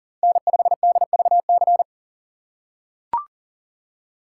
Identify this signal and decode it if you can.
Morse code, heard as a keyed tone, about 40 wpm, 700 Hz. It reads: N5DVC
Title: USA Call Signs - 5 Characters - 4x Character Spacing - 40wpm